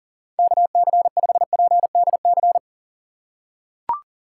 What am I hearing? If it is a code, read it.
Morse code, heard as a keyed tone, about 38 wpm, 700 Hz. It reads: KC5PDC
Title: USA Call Signs - 6 Characters - 3x Character Spacing - 40wpm